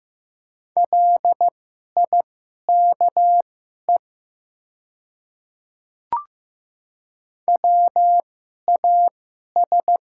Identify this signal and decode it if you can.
Morse code, heard as a keyed tone, about 15 wpm, spaced as for 11 wpm, 690 Hz. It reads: LIKE WAS
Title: 2x Character Spacing - Top 100 Words - 15wpm